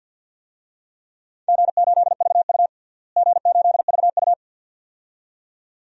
Morse code, heard as a keyed tone, about 50 wpm, 700 Hz. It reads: G8VU G8VU
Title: DXCC-100 Call Signs - 4 Characters - Repeated 2x - 50wpm